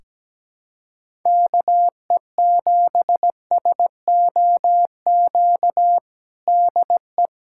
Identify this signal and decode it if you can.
Morse code, heard as a keyed tone, about 17 wpm, 700 Hz. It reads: KE7SOQ DE
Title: USA Callsign DE Callsign 17wpm